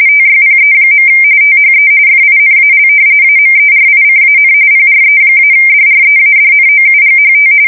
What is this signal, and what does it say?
RTTY, about 45.45 baud. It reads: TYPE. THE FINAL AMERICAN PRE-DREADNOUGHT
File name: fsk170 / battleships